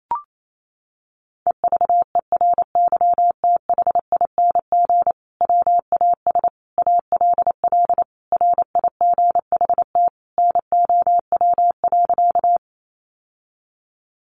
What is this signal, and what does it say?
Morse code, heard as a keyed tone, about 28 wpm, 700 Hz. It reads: EVERYTHING WAS ALL RIGHT NOW.
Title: Sentences from Top 700 Words 28wpm